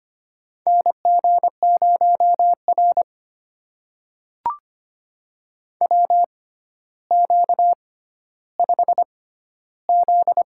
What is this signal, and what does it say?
Morse code, heard as a keyed tone, about 25 wpm, spaced as for 13 wpm, 700 Hz. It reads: NG0R WQ5Z
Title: USA Call Signs - 4 Characters - 6x Character Spacing - 25wpm